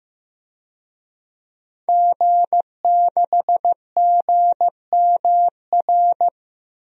Morse code, heard as a keyed tone, about 16 wpm, 700 Hz. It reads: G6GMR
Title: DXCC-100 Call Signs - 5 Characters - Repeated 2x - 15wpm